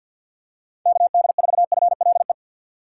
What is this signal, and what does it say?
Morse code, heard as a keyed tone, about 53 wpm, 700 Hz. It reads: KD4FLE
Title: USA Call Signs - 6 Characters - 3x Character Spacing - 50wpm